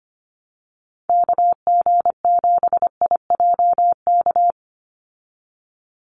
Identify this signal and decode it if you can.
Morse code, heard as a keyed tone, about 25 wpm, 700 Hz. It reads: KG7IJK
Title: USA Call Signs 25wpm